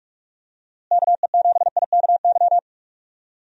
Morse code, heard as a keyed tone, about 41 wpm, 700 Hz. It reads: KE7IKY
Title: USA Call Signs - 6 Characters - Repeated 2x - 45wpm